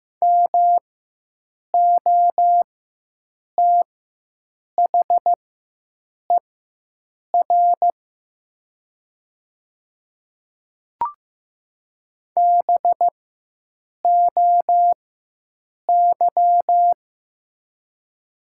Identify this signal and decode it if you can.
Morse code, heard as a keyed tone, about 15 wpm, spaced as for 7 wpm, 700 Hz. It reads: MOTHER BOY
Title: 4x Character Spacing - Top 500 Words - 15wpm